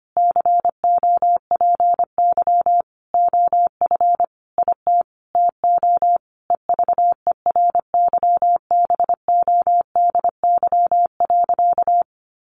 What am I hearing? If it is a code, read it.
Morse code, heard as a keyed tone, about 25 wpm, 700 Hz. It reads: COPY OF IT TO EVERYBODY.
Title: Sentences from Top 1400 Words 25wpm